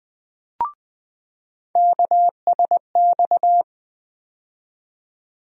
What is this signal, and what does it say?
Morse code, heard as a keyed tone, about 20 wpm, 700 Hz. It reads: KSX
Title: USA Call Sign Postfixes - 3 Letters 20wpm